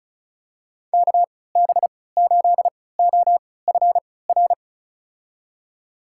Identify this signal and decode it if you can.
Morse code, heard as a keyed tone, about 34 wpm, spaced as for 20 wpm, 700 Hz. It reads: KB8OFR
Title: USA Call Signs - 6 Characters - 3x Character Spacing - 35wpm